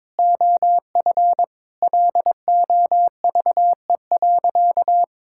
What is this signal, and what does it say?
Morse code, heard as a keyed tone, about 22 wpm, 700 Hz. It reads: OF LOVE.
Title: Sentences from Top 1000 Words - Review Part 5 - 22wpm